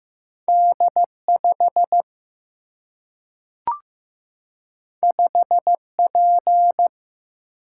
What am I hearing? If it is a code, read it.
Morse code, heard as a keyed tone, about 15 wpm, 700 Hz. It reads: D 5 5 P
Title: Single-Letter Single-Number 15wpm